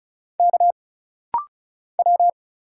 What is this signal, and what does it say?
Morse code, heard as a keyed tone, about 33 wpm, 700 Hz. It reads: K W
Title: Single Letter-Number Alt - Rapid-Fire 35wpm